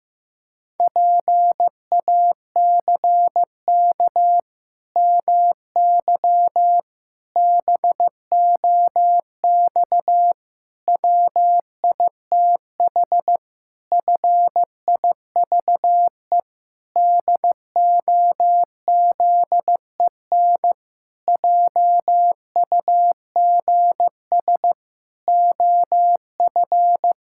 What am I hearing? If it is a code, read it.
Morse code, heard as a keyed tone, about 15 wpm, 700 Hz. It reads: PACK MY BOX WITH FIVE DOZEN JUGS OF
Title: Pangram Challenge v2 - 15wpm